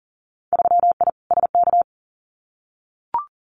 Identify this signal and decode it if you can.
Morse code, heard as a keyed tone, about 42 wpm, 700 Hz. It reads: 3I SK
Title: Winter Field Day Exchanges Encore 40wpm